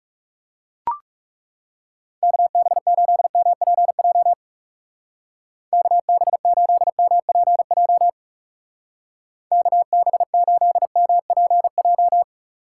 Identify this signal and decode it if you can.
Morse code, heard as a keyed tone, about 39 wpm, 700 Hz. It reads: KB8MPJ KB8MPJ KB8MPJ